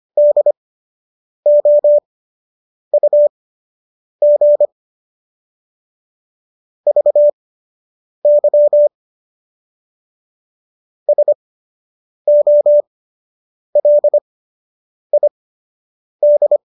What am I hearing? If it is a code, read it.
Morse code, heard as a keyed tone, about 25 wpm, spaced as for 8 wpm, 590 Hz. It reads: DOUG VY SOLID